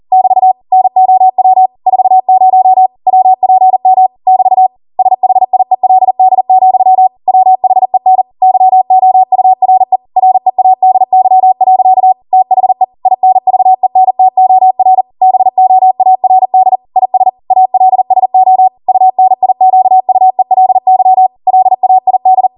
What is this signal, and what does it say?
Morse code, heard as a keyed tone, about 40 wpm, 750 Hz. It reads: <BT> NOW 40 WPM <BT> SHIELD, WHEN YOURE READY. THE INVENTOR BOARD IS ALSO UNIQUELY LAID